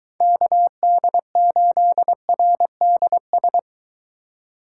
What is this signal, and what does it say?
Morse code, heard as a keyed tone, about 23 wpm, 700 Hz. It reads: KD8RDS